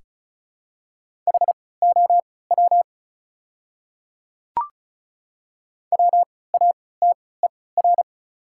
Morse code, heard as a keyed tone, about 36 wpm, spaced as for 20 wpm, 700 Hz. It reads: HOW WATER
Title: 3x Character Spacing - Top 100 Words - 35wpm